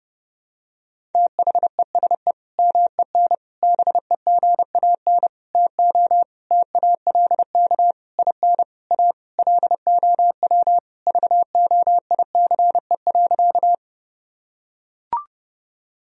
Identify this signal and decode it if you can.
Morse code, heard as a keyed tone, about 30 wpm, 700 Hz. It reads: THESE MEN BEGAN TO TALK IN A LOW VOICE.
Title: Sentences from Top 1000 Words - Review Part 1 - 30wpm